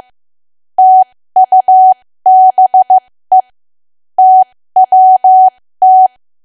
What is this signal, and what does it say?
Morse code, heard as a keyed tone, about 15 wpm, spaced as for 13 wpm, 750 Hz. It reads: TUBE TWT
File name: cwa / W1AW-Level2 / 13WPM